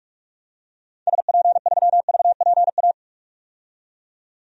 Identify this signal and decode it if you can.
Morse code, heard as a keyed tone, about 46 wpm, 700 Hz. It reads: SP3VPA
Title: DXCC-100 Call Signs - 6 Characters - 45wpm